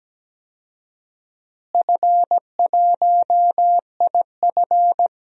Morse code, heard as a keyed tone, about 17 wpm, 700 Hz. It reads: F1IF
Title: DXCC-100 Call Signs - 4 Characters - Repeated 3x - 17wpm